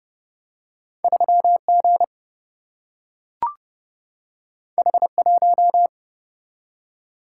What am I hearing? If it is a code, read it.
Morse code, heard as a keyed tone, about 30 wpm, 700 Hz. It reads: 3 G H 1
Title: Single-Letter Single-Number 30wpm